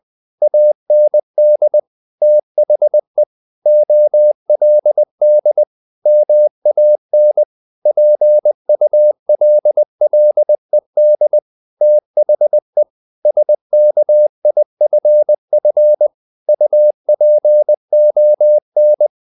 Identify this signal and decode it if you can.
Morse code, heard as a keyed tone, about 20 wpm, 600 Hz. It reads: AND THE OLD MAN PULLED THE SKIFF UPON
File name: the old man and the sea part 2 20wpm